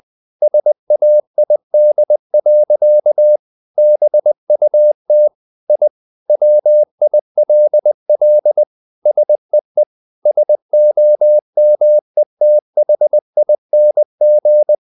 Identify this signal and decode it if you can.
Morse code, heard as a keyed tone, about 20 wpm, 600 Hz. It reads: SAID. BUT I WILL SEE SOMETHING
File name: the old man and the sea / the old man and the sea part 1 20wpm